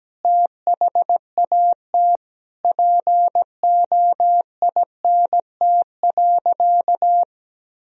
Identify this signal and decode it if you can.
Morse code, heard as a keyed tone, about 17 wpm, 700 Hz. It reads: THAT POINT.
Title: Sentences from Top 1000 Words - Review Part 1 - 17wpm